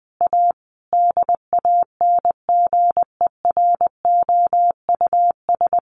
Morse code, heard as a keyed tone, about 20 wpm, 700 Hz. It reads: A DANGEROUS
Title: Sentences from Top 1500 Words 20wpm